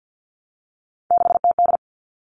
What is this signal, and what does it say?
Morse code, heard as a keyed tone, about 50 wpm, 700 Hz. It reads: 6TD
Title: All Cut Numbers - 3-Digit 50wpm